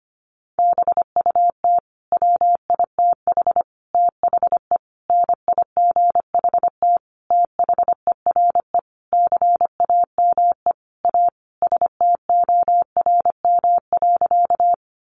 Morse code, heard as a keyed tone, about 25 wpm, 700 Hz. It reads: BUT WITH THE NIGHT THERE CAME A STORM.
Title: Sentences from Top 1500 Words 25wpm